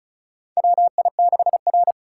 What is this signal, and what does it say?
Morse code, heard as a keyed tone, about 36 wpm, 700 Hz. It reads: WI6R
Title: USA Call Signs - 4 Characters - 4x Character Spacing - 35wpm